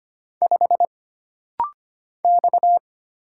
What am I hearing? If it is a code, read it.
Morse code, heard as a keyed tone, about 25 wpm, 700 Hz. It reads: 5 X
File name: Single Character (Letter:Number:Punctuation:Prosign) - Rapid-Fire - 25wpm